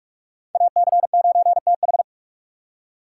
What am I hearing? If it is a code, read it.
Morse code, heard as a keyed tone, about 40 wpm, 700 Hz. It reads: AC9TH